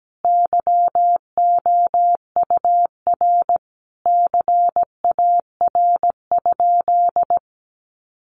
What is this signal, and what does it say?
Morse code, heard as a keyed tone, about 17 wpm, 700 Hz. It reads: YOUR CAR?
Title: Sentences from Top 1500 Words 17wpm